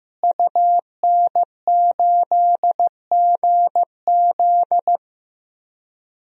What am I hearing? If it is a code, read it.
Morse code, heard as a keyed tone, about 15 wpm, 700 Hz. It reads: UN8GZ
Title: DXCC-100 Call Signs - 5 Characters - Repeated 3x - 15wpm